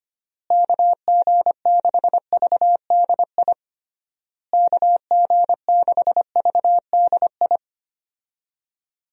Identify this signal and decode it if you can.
Morse code, heard as a keyed tone, about 25 wpm, 700 Hz. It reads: KG6VDI KG6VDI
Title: USA Call Signs - 6 Characters - Repeated 3x - 25wpm